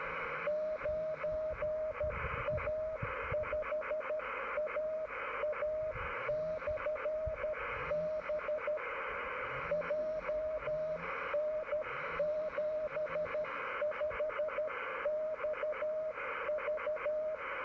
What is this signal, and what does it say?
Morse code, heard as a keyed tone, about 13 wpm, 610 Hz, with radio noise. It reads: 9A5AA/B JN75XV